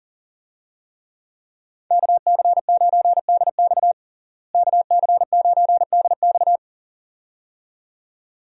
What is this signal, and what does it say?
Morse code, heard as a keyed tone, about 40 wpm, 700 Hz. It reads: KC9DX KC9DX
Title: USA Call Signs - 5 Characters - Repeated 2x - 40wpm